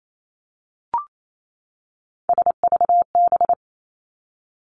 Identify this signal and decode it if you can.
Morse code, heard as a keyed tone, about 28 wpm, 700 Hz. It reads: SVB
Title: Three Challenging Letter-Number Combinations 28wpm